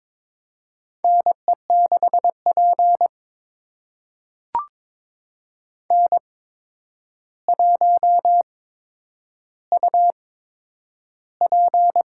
Morse code, heard as a keyed tone, about 23 wpm, spaced as for 10 wpm, 700 Hz. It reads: NE6P N1UP
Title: USA Call Signs - 8x Character Spacing - 22wpm